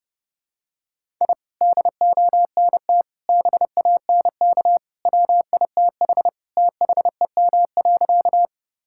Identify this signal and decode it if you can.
Morse code, heard as a keyed tone, about 30 wpm, 700 Hz. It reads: I DONT BANK WITH THEM.
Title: Sentences from Top 1000 Words - Review Part 6 - 30wpm